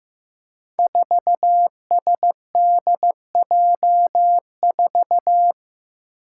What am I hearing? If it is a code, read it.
Morse code, heard as a keyed tone, about 15 wpm, 700 Hz. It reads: 4SDJ4
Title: Five Challenging Letter-Number Combinations 15wpm